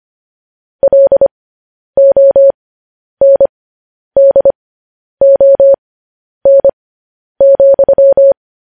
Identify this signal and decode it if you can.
Morse code, heard as a keyed tone, about 25 wpm, spaced as for 10 wpm, 550 Hz. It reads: LONDON,